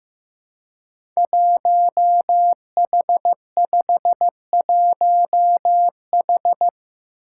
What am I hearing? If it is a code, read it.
Morse code, heard as a keyed tone, about 15 wpm, 700 Hz. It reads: 1H51H
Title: Five Challenging Letter-Number Combinations 15wpm